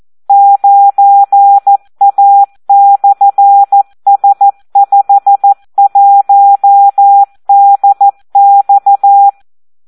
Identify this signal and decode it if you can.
Morse code, heard as a keyed tone, about 14 wpm, 800 Hz. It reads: 9A/S51DX
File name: Audio practica CW / SD 1 14